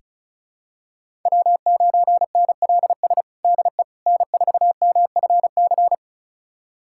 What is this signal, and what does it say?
Morse code, heard as a keyed tone, about 34 wpm, 700 Hz. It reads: W9NLS DE N4MFC
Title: USA Callsign DE Callsign 35wpm